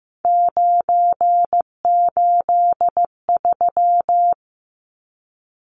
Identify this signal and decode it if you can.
Morse code, heard as a keyed tone, about 15 wpm, 700 Hz. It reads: 983
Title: Numbers - 3-Digit 15wpm